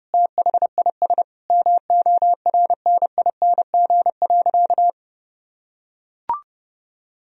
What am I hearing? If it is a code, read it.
Morse code, heard as a keyed tone, about 30 wpm, 700 Hz. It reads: THIS MORNING.
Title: Sentences from Top 1000 Words - Review Part 2 - 30wpm